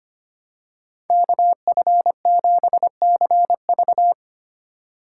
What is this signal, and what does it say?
Morse code, heard as a keyed tone, about 25 wpm, 700 Hz. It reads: KF7CV
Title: USA Call Signs - 4x Character Spacing - 25wpm